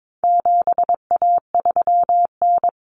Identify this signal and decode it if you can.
Morse code, heard as a keyed tone, about 22 wpm, 700 Hz. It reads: 7A3N
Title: Common Cut Numbers - 4-Digit 22wpm